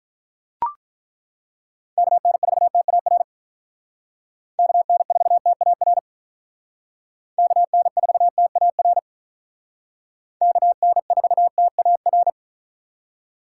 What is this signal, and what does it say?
Morse code, heard as a keyed tone, about 43 wpm, 700 Hz. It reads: KN4TAR KN4TAR KN4TAR KN4TAR
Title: Speed Racing - USA 2x3 Call Signs 35wpm